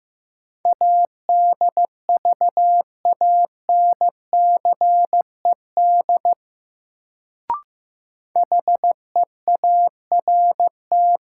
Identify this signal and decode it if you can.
Morse code, heard as a keyed tone, about 15 wpm, 700 Hz. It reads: ADVANCED HEART
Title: Sentences from Top 1200 Words - New Word Drill Alt - 15wpm